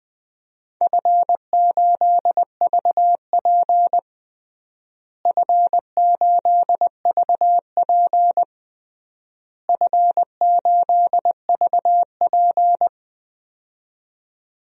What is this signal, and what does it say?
Morse code, heard as a keyed tone, about 21 wpm, 700 Hz. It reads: F8VP F8VP F8VP
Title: DXCC-100 Call Signs - 4 Characters - Repeated 3x - 20wpm